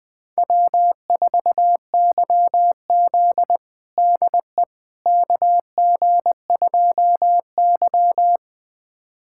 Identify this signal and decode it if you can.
Morse code, heard as a keyed tone, about 21 wpm, 700 Hz. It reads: W4YZ DE KG2Y
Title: USA Callsign DE Callsign - 4 Characters - 20wpm